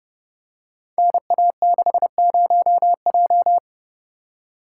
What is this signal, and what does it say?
Morse code, heard as a keyed tone, about 30 wpm, 700 Hz. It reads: NA60J
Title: Code Groups - 5 Letter:Numbers - 30wpm